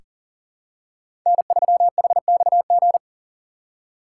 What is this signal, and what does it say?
Morse code, heard as a keyed tone, about 40 wpm, 700 Hz. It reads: N3HXG
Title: USA Call Signs - 5 Characters - 6x Character Spacing - 40wpm